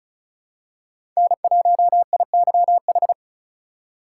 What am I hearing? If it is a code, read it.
Morse code, heard as a keyed tone, about 36 wpm, 700 Hz. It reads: N1IYH
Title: USA Call Signs - 5 Characters - 2x Character Spacing - 35wpm